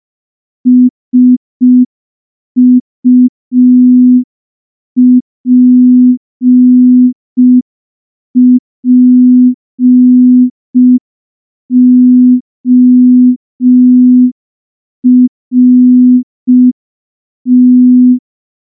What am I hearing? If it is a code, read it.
Morse code, heard as a keyed tone, about 5 wpm, 250 Hz. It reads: SUPPORT
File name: morse / morse-training-05WPM